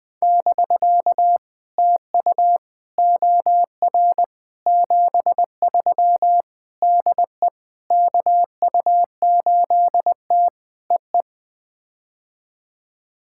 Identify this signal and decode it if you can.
Morse code, heard as a keyed tone, about 20 wpm, 700 Hz. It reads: <BK> TU OR 73 DE KU8T EE
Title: POTA QSO Practice - 20wpm